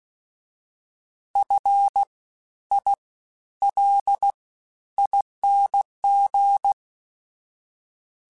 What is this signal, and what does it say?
Morse code, heard as a keyed tone, about 16 wpm, 800 Hz. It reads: F I L ING
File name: Special INGs-mp3s / Special INGs-mp3s / SlowToFaster ing @15